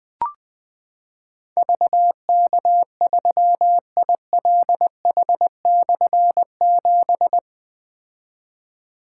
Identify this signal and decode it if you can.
Morse code, heard as a keyed tone, about 20 wpm, 700 Hz. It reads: VK3ILH/7
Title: Difficult Call Signs 20wpm